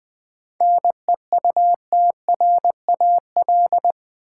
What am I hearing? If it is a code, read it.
Morse code, heard as a keyed tone, about 20 wpm, 700 Hz. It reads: NEUTRAL
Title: 7 Letter Words - First 3 Letters Given - 20wpm